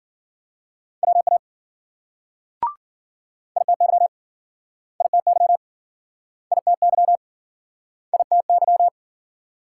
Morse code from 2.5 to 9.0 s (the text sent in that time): ITY ITY ITY ITY